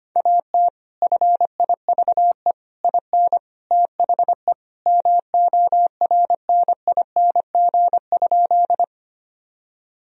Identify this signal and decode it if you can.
Morse code, heard as a keyed tone, about 25 wpm, 700 Hz. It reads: AT FIVE IN THE MORNING?
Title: Sentences from Top 1000 Words - Review Part 2 - 25wpm